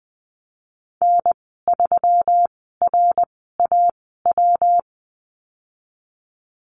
Morse code, heard as a keyed tone, about 20 wpm, spaced as for 15 wpm, 700 Hz. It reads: N3RAW
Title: 193 Lesson 31g - Letter v - Call Signs 20wpm